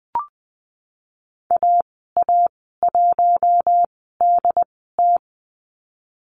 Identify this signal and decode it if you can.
Morse code, heard as a keyed tone, about 20 wpm, spaced as for 14 wpm, 700 Hz. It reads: AA1DT
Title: 088 Lesson 15g - Number 5 - Call Signs 20wpm